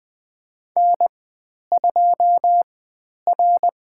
Morse code, heard as a keyed tone, about 20 wpm, spaced as for 10 wpm, 700 Hz. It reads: N2R
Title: 095 Lesson 16g - Letter c - Call Signs 20wpm